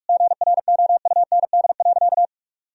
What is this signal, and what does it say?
Morse code, heard as a keyed tone, about 46 wpm, 700 Hz. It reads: GROUND.
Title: Sentences from Top 1000 Words - Review Part 2 - 45wpm